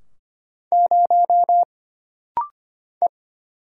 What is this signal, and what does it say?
Morse code, heard as a keyed tone, about 25 wpm, 700 Hz. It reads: 0 E
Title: Single Letter-Number - Rapid-Fire 25wpm